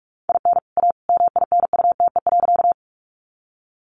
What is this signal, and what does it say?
Morse code, heard as a keyed tone, about 44 wpm, 700 Hz. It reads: IN A MINUTE.
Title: Sentences from Top 500 Words 45wpm